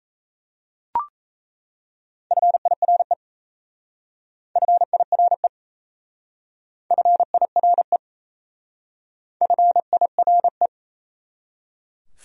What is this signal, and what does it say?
Morse code, heard as a keyed tone, about 34 wpm, 690 Hz. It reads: FIRE FIRE FIRE FIRE
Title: Speed-Racing Top 500 Words 28wpm